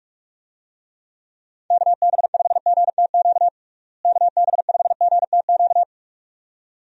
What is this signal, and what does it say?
Morse code, heard as a keyed tone, about 46 wpm, 700 Hz. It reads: KB5GTQ KB5GTQ
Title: USA Call Signs - 6 Characters - Repeated 2x - 45wpm